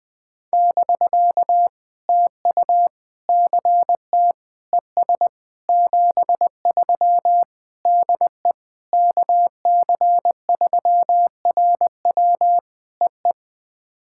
Morse code, heard as a keyed tone, about 18 wpm, 700 Hz. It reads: <BK> TU CT ES 73 DE KC3RW EE
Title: POTA QSO Practice - 20wpm